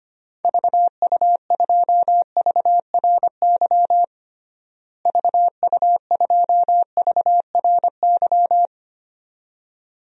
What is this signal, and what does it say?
Morse code, heard as a keyed tone, about 25 wpm, 700 Hz. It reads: VU2VRY VU2VRY
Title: DXCC-100 Call Signs - 6 Characters - Repeated 2x - 25wpm